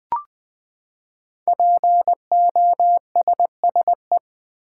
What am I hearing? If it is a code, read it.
Morse code, heard as a keyed tone, about 20 wpm, 700 Hz. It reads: POSSE
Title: Build A Word - 5 Letters At A Time - 10-Letter Words - 20wpm